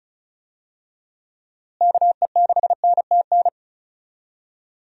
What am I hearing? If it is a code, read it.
Morse code, heard as a keyed tone, about 36 wpm, 700 Hz. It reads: KE6NTN
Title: USA Call Signs - 6 Characters - Repeated 3x - 35wpm